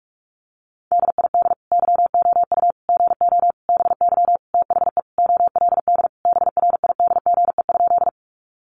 Morse code, heard as a keyed tone, about 44 wpm, 700 Hz. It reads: DID YOU GO BY THE OLD BRIDGE?
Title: Sentences from Top 1200 Words 45wpm